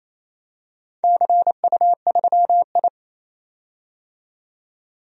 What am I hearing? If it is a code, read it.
Morse code, heard as a keyed tone, about 27 wpm, 690 Hz. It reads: CU3I
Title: DXCC-100 Call Signs - 4 Characters - 28wpm